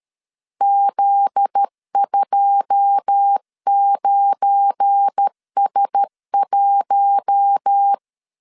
Morse code, heard as a keyed tone, about 13 wpm, 800 Hz. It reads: Z29S1
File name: LEZIONI / 9 2 12wpm 2301